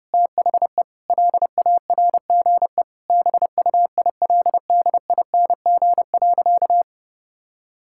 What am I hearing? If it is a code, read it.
Morse code, heard as a keyed tone, about 30 wpm, 700 Hz. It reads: THE LARGE BUILDING.
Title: Sentences from Top 1000 Words - Review Part 5 - 30wpm